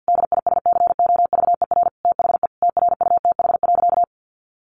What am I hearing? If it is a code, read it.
Morse code, heard as a keyed tone, about 50 wpm, 700 Hz. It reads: DISCOVER THE TRUTH.
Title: Sentences from Top 1100 Words 50wpm